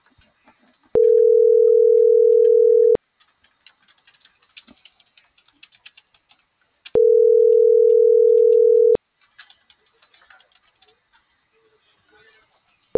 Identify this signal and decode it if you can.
Ringback tone (North America), 440 + 480 Hz.
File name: CENTRAL GENERAL / out erick.mendoza.acv1